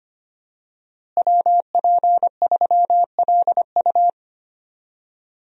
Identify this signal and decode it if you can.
Morse code, heard as a keyed tone, about 25 wpm, 700 Hz. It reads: WP3LU
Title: USA Call Signs - 5 Characters - Repeated 2x - 25wpm